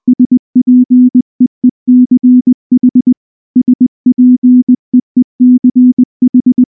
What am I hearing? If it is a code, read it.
Morse code, heard as a keyed tone, about 20 wpm, 250 Hz. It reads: SPEECH SPEECH
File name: morse-training-20WPM